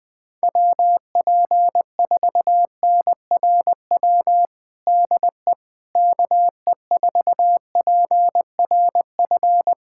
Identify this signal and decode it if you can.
Morse code, heard as a keyed tone, about 20 wpm, 700 Hz. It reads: WP4NRW DE KE4PRF
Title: USA Callsign DE Callsign - 6 Characters - 20wpm